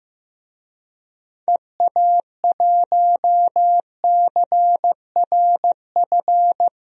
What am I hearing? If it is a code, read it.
Morse code, heard as a keyed tone, about 15 wpm, 700 Hz. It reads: EA1CRF
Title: DXCC-100 Call Signs - 6 Characters - Repeated 2x - 15wpm